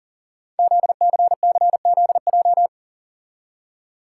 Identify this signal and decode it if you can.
Morse code, heard as a keyed tone, about 42 wpm, 700 Hz. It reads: ZCCZJ
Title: Code Groups - 5 Letters - 40wpm